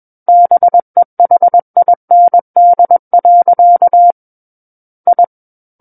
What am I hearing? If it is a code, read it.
Morse code, heard as a keyed tone, about 21 wpm, 700 Hz. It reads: BEHIND. I
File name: Book One / 12-Chapter 12 - 21 WPM